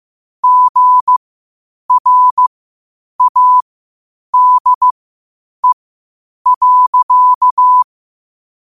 Morse code, heard as a keyed tone, about 15 wpm, spaced as for 8 wpm, 1000 Hz. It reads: GRADE.
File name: qotdfiles / 8wpm